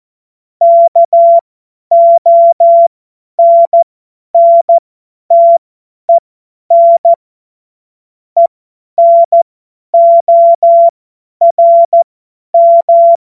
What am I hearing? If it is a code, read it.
Morse code, heard as a keyed tone, about 14 wpm, spaced as for 10 wpm, 680 Hz. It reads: KONNTEN ENORM
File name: Morsekurs / Witze / Witz 06. Januar 2026 50 F70